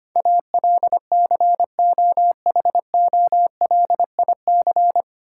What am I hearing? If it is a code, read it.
Morse code, heard as a keyed tone, about 25 wpm, 700 Hz. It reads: ALCOHOLIC
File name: Nine Letter Words - 25wpm